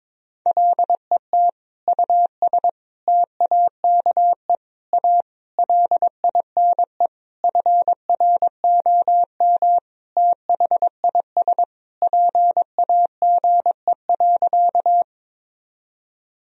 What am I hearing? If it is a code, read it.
Morse code, heard as a keyed tone, about 22 wpm, 700 Hz. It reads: LET US TAKE A LINE FROM THIS PAGE.
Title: Sentences from Top 1000 Words - Review Part 1 - 22wpm